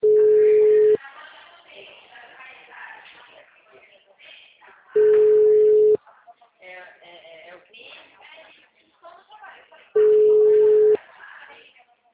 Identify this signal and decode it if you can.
Ringback tone (Europe, 425 Hz), with speech.